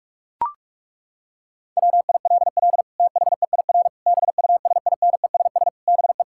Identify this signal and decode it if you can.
Morse code, heard as a keyed tone, about 42 wpm, 700 Hz. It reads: WILL THEIR BUSINESS BE
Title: Sentences from Top 1000 Words - Review Part 5 - 45wpm